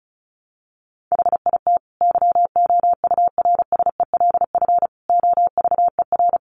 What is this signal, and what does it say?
Morse code, heard as a keyed tone, about 34 wpm, 700 Hz. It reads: HIT YOURSELF OVER
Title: Sentences from Top 1200 Words 35wpm